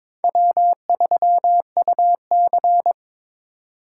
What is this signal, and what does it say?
Morse code, heard as a keyed tone, about 26 wpm, 700 Hz. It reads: W3UC
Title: Speed Racing - USA Call Signs 22wpm